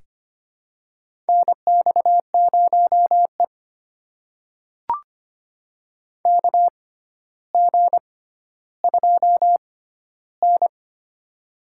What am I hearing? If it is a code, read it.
Morse code, heard as a keyed tone, about 25 wpm, spaced as for 13 wpm, 700 Hz. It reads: NX0E KG2N
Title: USA Call Signs - 4 Characters - 6x Character Spacing - 25wpm